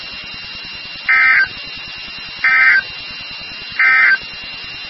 Emergency Alert System (SAME) tones, heard as harsh buzzing bursts. It is heard from 1.0 to 4.2 s.
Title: NOAA Weather Radio WXK36